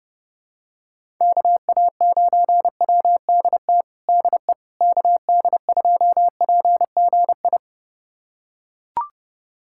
Morse code, heard as a keyed tone, about 29 wpm, 700 Hz. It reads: KA9WDT DE KD2PGI
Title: USA Callsign DE Callsign - 6 Characters - 30wpm